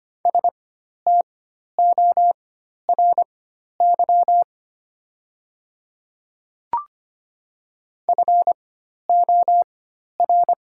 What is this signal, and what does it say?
Morse code, heard as a keyed tone, about 25 wpm, spaced as for 12 wpm, 690 Hz. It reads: STORY FOR
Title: 4x Character Spacing - Top 300 Words - 25wpm